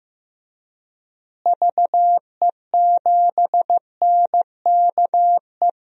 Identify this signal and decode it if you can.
Morse code, heard as a keyed tone, about 15 wpm, 700 Hz. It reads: VE7NKE
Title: DXCC-100 Call Signs - 6 Characters - Repeated 3x - 15wpm